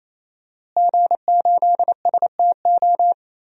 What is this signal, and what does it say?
Morse code, heard as a keyed tone, about 27 wpm, 700 Hz. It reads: G8STO
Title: DXCC-100 Call Signs - 5 Characters - Repeated 3x - 28wpm